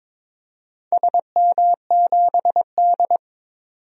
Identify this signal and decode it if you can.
Morse code, heard as a keyed tone, about 22 wpm, 700 Hz. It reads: SM7D
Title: DXCC-100 Call Signs - 4 Characters - 22wpm